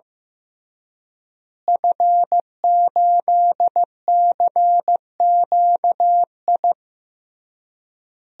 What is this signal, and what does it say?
Morse code, heard as a keyed tone, about 15 wpm, 700 Hz. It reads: F8CQI
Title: DXCC-100 Call Signs - 5 Characters - Repeated 3x - 15wpm